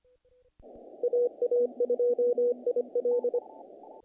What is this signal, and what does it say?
Morse code, heard as a keyed tone, about 25 wpm, 490 Hz. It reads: AA2IL
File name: contests / naqp-cw-k5tr-op-k5pi- -1800